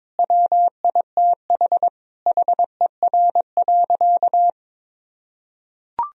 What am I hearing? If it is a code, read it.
Morse code, heard as a keyed tone, about 22 wpm, 700 Hz. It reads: WITH HER.
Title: Sentences from Top 1000 Words - Review Part 4 - 22wpm